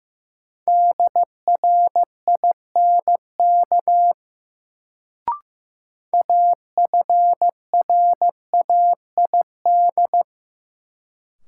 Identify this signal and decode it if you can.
Morse code, heard as a keyed tone, about 16 wpm, 700 Hz. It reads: DRINK AFRAID
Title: Sentences from Top 1500 Words - New Word Drill - 15wpm